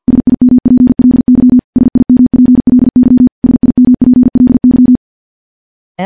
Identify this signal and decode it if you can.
Morse code, heard as a keyed tone, about 51 wpm, 250 Hz. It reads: SIMPLY SIMPLY SIMPLY
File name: morse / morse-training-50WPM